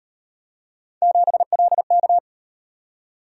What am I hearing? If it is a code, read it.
Morse code, heard as a keyed tone, about 38 wpm, 700 Hz. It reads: 7LK